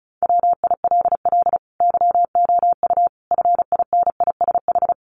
Morse code, heard as a keyed tone, about 34 wpm, 700 Hz. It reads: WILL YOU FINISH
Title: Sentences from Top 1300 Words 35wpm